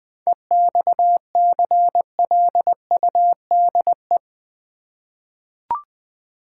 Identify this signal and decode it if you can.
Morse code, heard as a keyed tone, about 20 wpm, 690 Hz. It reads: EXCLUDE
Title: 7 Letter Words - First 3 Letters Given - 20wpm